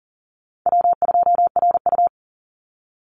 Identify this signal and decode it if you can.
Morse code, heard as a keyed tone, about 40 wpm, 700 Hz. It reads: W2RU
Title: Easy Call Signs - 40wpm